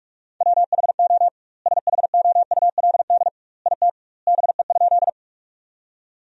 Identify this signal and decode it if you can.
Morse code, heard as a keyed tone, about 46 wpm, 700 Hz. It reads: WHO SHOULD IT BE?
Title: 2x Word-Spacing - Sentences from Top 300 Words - 45wpm